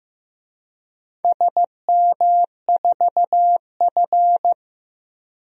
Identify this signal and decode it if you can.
Morse code, heard as a keyed tone, about 16 wpm, 700 Hz. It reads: SM4F
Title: DXCC-100 Call Signs - 4 Characters - Repeated 3x - 15wpm